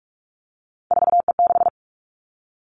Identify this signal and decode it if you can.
Morse code, heard as a keyed tone, about 46 wpm, 700 Hz. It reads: 4E6
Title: Three Challenging Letter-Number Combinations 45wpm